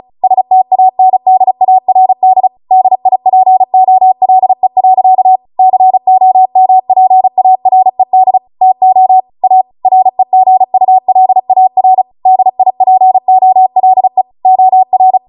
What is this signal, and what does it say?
Morse code, heard as a keyed tone, about 36 wpm, 750 Hz. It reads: STANDARD DIPOLE. COMPARED TO A REGULAR DIPOLE OR
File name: Morse / 35WPM